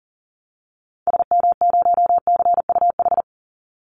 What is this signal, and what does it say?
Morse code, heard as a keyed tone, about 40 wpm, 700 Hz. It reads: SM0CUH
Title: Call Signs Repeated 2x - 40wpm